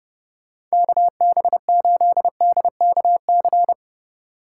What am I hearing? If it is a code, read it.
Morse code, heard as a keyed tone, about 29 wpm, 700 Hz. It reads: KB8DKC